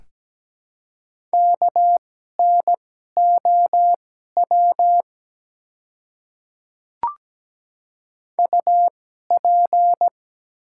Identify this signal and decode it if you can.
Morse code, heard as a keyed tone, about 17 wpm, spaced as for 12 wpm, 700 Hz. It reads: KNOW UP
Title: 2x Character Spacing - Top 200 Words - 17wpm